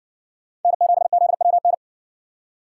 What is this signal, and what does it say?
Morse code, heard as a keyed tone, about 60 wpm, 700 Hz. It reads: N7ZWN